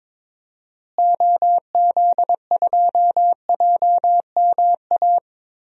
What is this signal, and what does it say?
Morse code, heard as a keyed tone, about 21 wpm, 700 Hz. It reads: OZ2JMA
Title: DXCC-100 Call Signs - 6 Characters - Repeated 3x - 22wpm